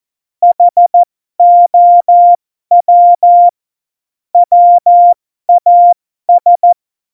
Morse code, heard as a keyed tone, about 14 wpm, spaced as for 12 wpm, 700 Hz. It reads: HOW WAS
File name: Morsekurs / Witze